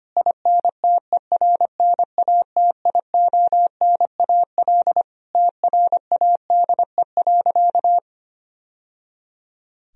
Morse code, heard as a keyed tone, about 25 wpm, 700 Hz. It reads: INTERNATIONAL TRADE.
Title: Sentences from Top 1000 Words - Review Part 4 - 25wpm